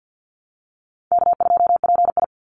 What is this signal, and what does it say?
Morse code, heard as a keyed tone, about 50 wpm, 700 Hz. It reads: K2PI
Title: Easy Call Signs Repeated 2x - 50wpm